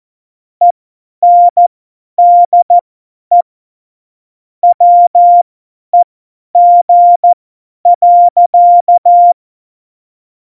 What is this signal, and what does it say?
Morse code, heard as a keyed tone, about 14 wpm, spaced as for 10 wpm, 700 Hz. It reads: ENDE WEG.
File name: Morsekurs / Witze / Witz 19. Januar 2021, 50 F70